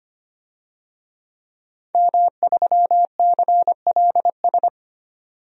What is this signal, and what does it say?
Morse code, heard as a keyed tone, about 25 wpm, 700 Hz. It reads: M3CLS
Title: DXCC-100 Call Signs - 5 Characters - 25wpm